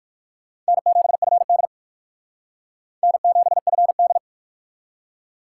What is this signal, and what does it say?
Morse code, heard as a keyed tone, about 56 wpm, 700 Hz. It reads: N7FD N7FD
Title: Speed Racing - USA 1x2 Call Signs 45wpm